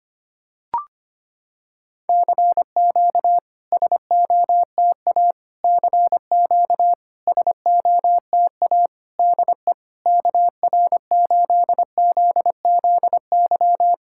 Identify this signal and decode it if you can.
Morse code, heard as a keyed tone, about 25 wpm, 700 Hz. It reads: CQ SOTA CQ SOTA DE KR8ZZY
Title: SOTA QSO Practice 25wpm